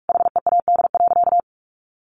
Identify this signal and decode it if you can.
Morse code, heard as a keyed tone, about 44 wpm, 700 Hz. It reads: HEAD.
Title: Sentences from Top 1100 Words 45wpm